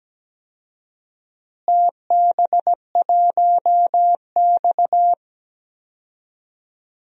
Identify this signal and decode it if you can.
Morse code, heard as a keyed tone, about 17 wpm, 700 Hz. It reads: TB1X
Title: DXCC-100 Call Signs - 4 Characters - Repeated 2x - 17wpm